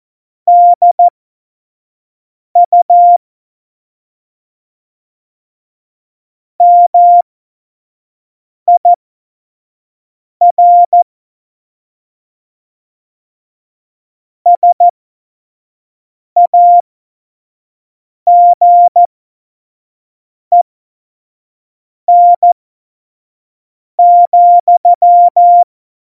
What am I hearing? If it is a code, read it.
Morse code, heard as a keyed tone, about 14 wpm, spaced as for 5 wpm, 700 Hz. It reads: DU MIR SAGEN,
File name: Morsekurs / Witze / Witz 23. Juli 2019, 25 F70